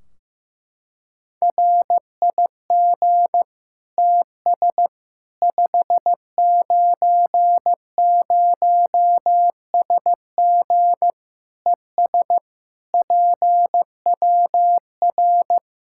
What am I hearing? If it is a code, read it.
Morse code, heard as a keyed tone, about 16 wpm, 700 Hz. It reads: RIG TS 590SG ES PWR
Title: K1USN SST QSO Practice 15wpm